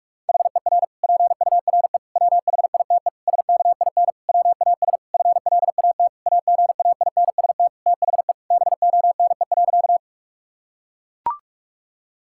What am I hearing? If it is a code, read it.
Morse code, heard as a keyed tone, about 41 wpm, 700 Hz. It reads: HER PURE WHITE SKIN WAS FLAT AGAINST THE BONE.
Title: Sentences from Top 1600 Words 45wpm